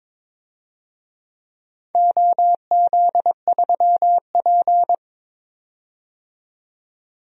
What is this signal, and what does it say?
Morse code, heard as a keyed tone, about 23 wpm, 700 Hz. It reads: OZ3P
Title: DXCC-100 Call Signs - 4 Characters - 22wpm